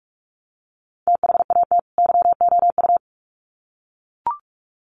Morse code, heard as a keyed tone, about 44 wpm, 700 Hz. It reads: THAT YOU
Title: Sets of 2 Words 45wpm